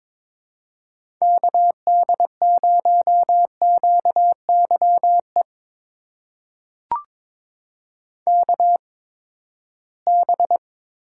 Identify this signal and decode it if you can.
Morse code, heard as a keyed tone, about 22 wpm, spaced as for 15 wpm, 700 Hz. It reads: KD0QYE KB
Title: USA Call Signs - 6 Characters - 8x Character Spacing - 22wpm